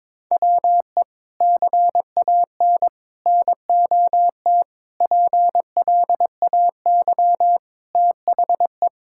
Morse code, heard as a keyed tone, about 22 wpm, 700 Hz. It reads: WE CAN NOT PLAY THE
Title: Sentences from Top 1000 Words - Review Part 6 - 22wpm